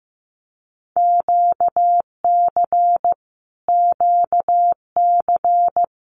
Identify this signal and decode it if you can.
Morse code, heard as a keyed tone, about 15 wpm, 700 Hz. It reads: QC QC
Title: Canadian Provinces and Territories - Repeated 3x - 15wpm